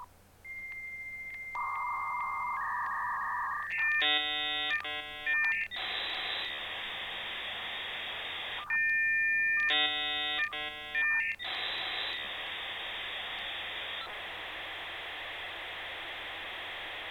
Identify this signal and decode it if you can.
Dial-up modem handshake.